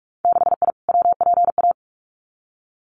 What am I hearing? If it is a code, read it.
Morse code, heard as a keyed tone, about 46 wpm, 700 Hz. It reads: 6I WPA
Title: Winter Field Day Exchanges Encore 45wpm